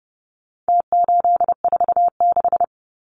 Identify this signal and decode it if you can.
Morse code, heard as a keyed tone, about 30 wpm, 700 Hz. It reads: T846
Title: Numbers - Contest Style - 4-Digit 30wpm